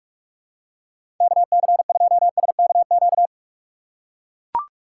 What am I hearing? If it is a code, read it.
Morse code, heard as a keyed tone, about 43 wpm, 700 Hz. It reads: KC2SKQ
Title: USA Call Signs - 4x Character Spacing - 45wpm